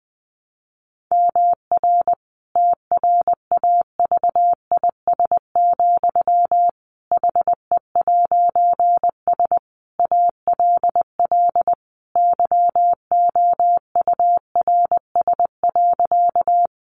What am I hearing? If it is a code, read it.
Morse code, heard as a keyed tone, about 20 wpm, 700 Hz. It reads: MR TRAVIS, HE'S ALL YOURS.
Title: A Sound Of Thunder 20wpm